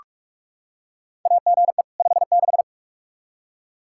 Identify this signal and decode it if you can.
Morse code, heard as a keyed tone, about 46 wpm, 700 Hz. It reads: AGE 56
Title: K1USN SST QSO Practice 45wpm